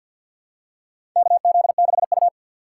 Morse code, heard as a keyed tone, about 50 wpm, 700 Hz. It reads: KZ6U
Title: USA Call Signs - 4x Character Spacing - 50wpm